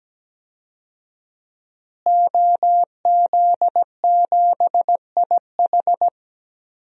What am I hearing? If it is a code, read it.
Morse code, heard as a keyed tone, about 17 wpm, 700 Hz. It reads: OZ7IH
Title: DXCC-100 Call Signs - 5 Characters - Repeated 2x - 17wpm